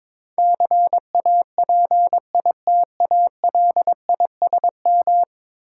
Morse code, heard as a keyed tone, about 22 wpm, 700 Hz. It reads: CAPITALISM
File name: practice / 22wpm / 10 Letter Words - First 3 Letters Given - 22wpm